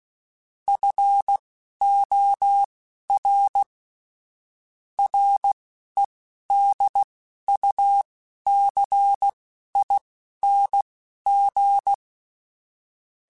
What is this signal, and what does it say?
Morse code, heard as a keyed tone, about 16 wpm, spaced as for 11 wpm, 800 Hz. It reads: FOR REDUCING
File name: QSOs / QSOs-mp3s / WD0FBT 1@15